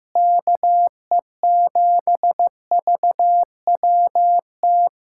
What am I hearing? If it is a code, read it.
Morse code, heard as a keyed tone, about 15 wpm, 700 Hz. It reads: KE7VWT
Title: USA Callsign DE Callsign - 6 Characters - 15wpm